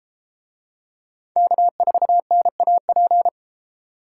Morse code, heard as a keyed tone, about 31 wpm, 700 Hz. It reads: K4NAP